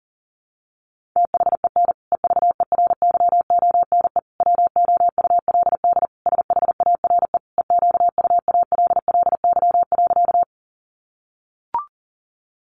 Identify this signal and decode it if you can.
Morse code, heard as a keyed tone, about 40 wpm, 700 Hz. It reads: THEN EVERYONE WOULD SHARE EQUALLY.
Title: Sentences from Top 1400 Words 40wpm